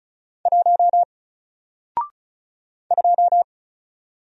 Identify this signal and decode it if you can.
Morse code, heard as a keyed tone, about 36 wpm, 700 Hz. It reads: 1 2
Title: Single Letter-Number Alt 35wpm